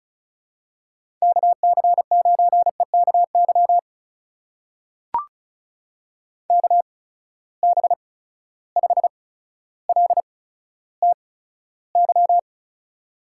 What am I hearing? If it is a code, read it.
Morse code, heard as a keyed tone, about 36 wpm, spaced as for 15 wpm, 700 Hz. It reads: KC9EKY KB5LTY
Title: USA Call Signs - 6 Characters - 8x Character Spacing - 35wpm